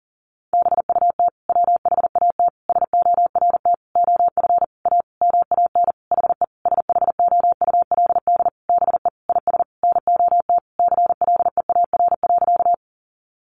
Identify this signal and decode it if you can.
Morse code, heard as a keyed tone, about 40 wpm, 700 Hz. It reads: BUT WHAT SORT OF A MAN HE SHOULD BE IS NOT CLEAR.
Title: Sentences from Top 1300 Words 40wpm